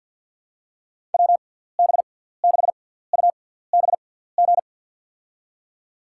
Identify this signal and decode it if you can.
Morse code, heard as a keyed tone, about 52 wpm, spaced as for 17 wpm, 700 Hz. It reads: WB6UBG
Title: USA Call Signs - 6 Characters - 6x Character Spacing - 50wpm